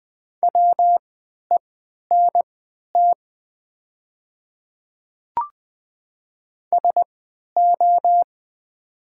Morse code, heard as a keyed tone, about 20 wpm, spaced as for 11 wpm, 700 Hz. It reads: WENT SO
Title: 3x Character Spacing - Top 300 Words - 20wpm